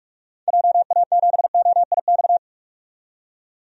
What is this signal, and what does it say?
Morse code, heard as a keyed tone, about 42 wpm, 700 Hz. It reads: JA7OIX
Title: DXCC-100 Call Signs - 6 Characters - 45wpm